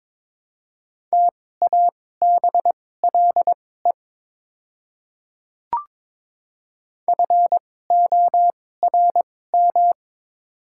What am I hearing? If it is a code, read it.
Morse code, heard as a keyed tone, about 22 wpm, spaced as for 16 wpm, 690 Hz. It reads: TABLE FORM
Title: 2x Character Spacing - Top 500 Words - 22wpm